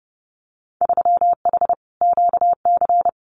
Morse code, heard as a keyed tone, about 30 wpm, 700 Hz. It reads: 3H QC
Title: Winter Field Day Exchanges Encore 30wpm